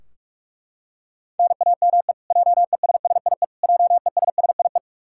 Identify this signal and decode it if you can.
Morse code, heard as a keyed tone, about 42 wpm, 700 Hz. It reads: NAME JESSIE JESSIE
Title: Practice QSO - Exchange 1 - Beginning - 45wpm